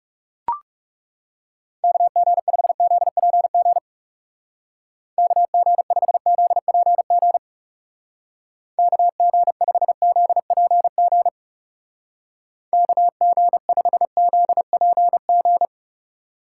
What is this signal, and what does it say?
Morse code, heard as a keyed tone, about 37 wpm, 700 Hz. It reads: KG5ZPG KG5ZPG KG5ZPG KG5ZPG
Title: Speed Racing - USA 2x3 Call Signs 30wpm